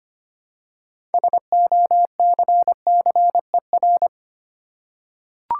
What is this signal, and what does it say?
Morse code, heard as a keyed tone, about 25 wpm, 700 Hz. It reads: SOCCER
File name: practice / 25wpm / Easy Words with Hints - 25wpm